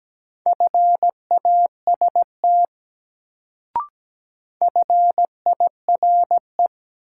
Morse code, heard as a keyed tone, about 17 wpm, 700 Hz. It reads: FAST FIRE
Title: Sentences from Top 400 Words - New Word Drill - 17wpm